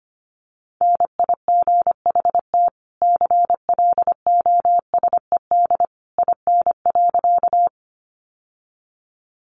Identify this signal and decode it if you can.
Morse code, heard as a keyed tone, about 25 wpm, 700 Hz. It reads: NIGHT CLOSED IN.
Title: Sentences from Top 1400 Words 25wpm